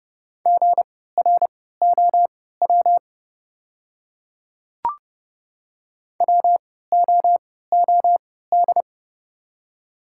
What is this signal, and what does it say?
Morse code, heard as a keyed tone, about 29 wpm, spaced as for 17 wpm, 690 Hz. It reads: GROW WOOD
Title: 3x Character Spacing - Top 300 Words - 30wpm